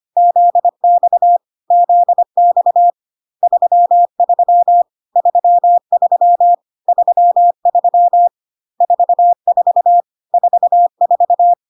Morse code, heard as a keyed tone, about 25 wpm, 700 Hz. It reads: ZX ZX 33 33 33 44 44
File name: CW abbreviations alphabetical repeated 3 times 25 25wpm